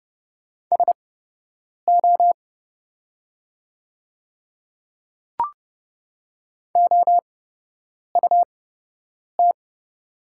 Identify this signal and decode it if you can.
Morse code, heard as a keyed tone, about 30 wpm, spaced as for 8 wpm, 700 Hz. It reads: SO OUT
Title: 8x Character Spacing - Top 100 Words - 30wpm